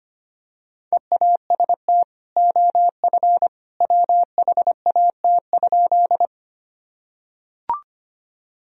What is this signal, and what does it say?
Morse code, heard as a keyed tone, about 25 wpm, 700 Hz. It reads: EAST OF WHAT?
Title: Sentences from Top 1000 Words - Review Part 3 - 25wpm